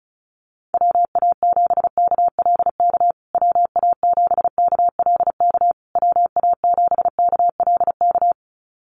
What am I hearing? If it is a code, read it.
Morse code, heard as a keyed tone, about 34 wpm, 700 Hz. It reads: WA7KLK WA7KLK WA7KLK
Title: USA Call Signs Repeated 3x 35wpm